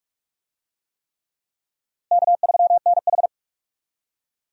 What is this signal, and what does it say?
Morse code, heard as a keyed tone, about 42 wpm, 700 Hz. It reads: K3NH
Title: USA Call Signs - 4 Characters - Repeated 3x - 45wpm